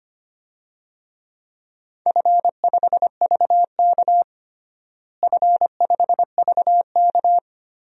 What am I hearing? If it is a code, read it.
Morse code, heard as a keyed tone, about 25 wpm, 700 Hz. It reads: F5VK F5VK
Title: DXCC-100 Call Signs - 4 Characters - Repeated 2x - 25wpm